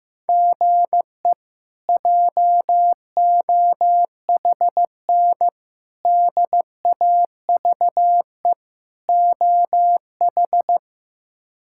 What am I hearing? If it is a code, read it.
Morse code, heard as a keyed tone, about 15 wpm, 700 Hz. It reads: GE JOHN DAVE OH
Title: K1USN SST QSO Practice - 15wpm